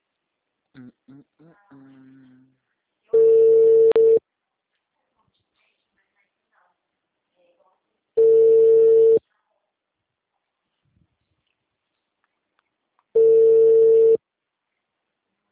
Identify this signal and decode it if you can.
Ringback tone (China, 450 Hz), with speech.
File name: out- -2017- - - .8876